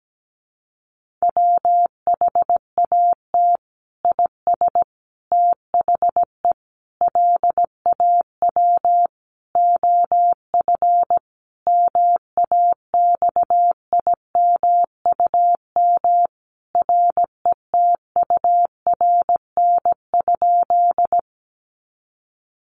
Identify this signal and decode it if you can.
Morse code, heard as a keyed tone, about 17 wpm, 700 Hz. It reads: WHAT IS THE LAW OF MAXIMUM RETURN?
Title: Sentences from Top 1300 Words 17wpm